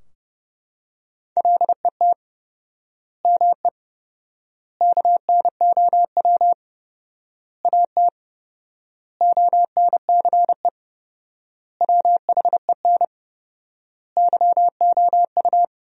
Morse code, heard as a keyed tone, about 30 wpm, 700 Hz. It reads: LET ME KNOW AT ONCE WHEN YOU
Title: 4x Word-Spacing - Sentences from Top 300 Words - 30wpm